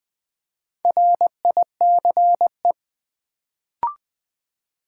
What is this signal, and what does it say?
Morse code, heard as a keyed tone, about 20 wpm, 700 Hz. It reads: RICE
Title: Build A Word - 1 Letter At A Time - 4-Letter Words - 20wpm